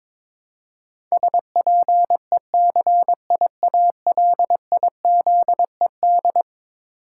Morse code, heard as a keyed tone, about 22 wpm, 700 Hz. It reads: SPECIALIZED
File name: practice / 22wpm / Eleven Letter Words - 22wpm